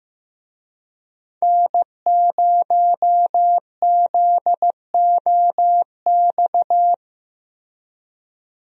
Morse code, heard as a keyed tone, about 15 wpm, 700 Hz. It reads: N0ZOX
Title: USA Call Signs - 5 Characters - Repeated 2x - 15wpm